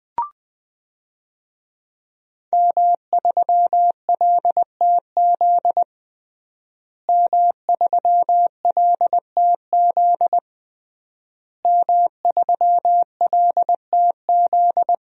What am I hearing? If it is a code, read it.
Morse code, heard as a keyed tone, about 21 wpm, 700 Hz. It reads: M3LTZ M3LTZ M3LTZ
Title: DXCC-100 Call Signs - 5 Characters - Repeated 3x - 20wpm